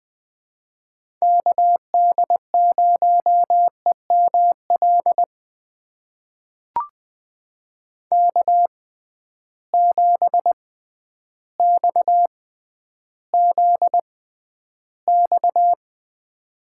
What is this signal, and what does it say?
Morse code, heard as a keyed tone, about 20 wpm, spaced as for 11 wpm, 700 Hz. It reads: KD0EML K7XZX
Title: USA Call Signs - 6x Character Spacing - 20wpm